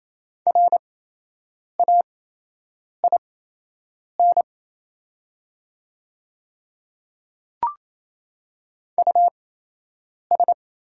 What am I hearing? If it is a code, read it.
Morse code, heard as a keyed tone, about 28 wpm, spaced as for 8 wpm, 700 Hz. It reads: RAIN US
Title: 8x Character Spacing - Top 300 Words - 28wpm